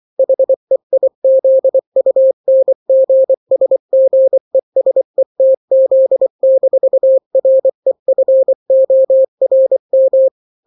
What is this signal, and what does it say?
Morse code, heard as a keyed tone, about 24 wpm, 530 Hz. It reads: HEIZUNGSGESETZ-REFORM
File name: daily / 23wpm